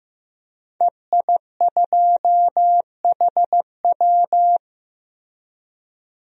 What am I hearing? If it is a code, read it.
Morse code, heard as a keyed tone, about 15 wpm, 700 Hz. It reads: EI2HW
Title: DXCC-100 Call Signs - 5 Characters - Repeated 3x - 15wpm